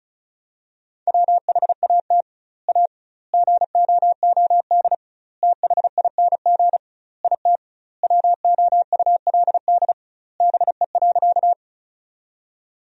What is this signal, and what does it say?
Morse code, heard as a keyed tone, about 34 wpm, 700 Hz. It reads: WHAT A GOOD THING IT WOULD BE.
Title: 2x Word-Spacing - Sentences from Top 200 Words - 35wpm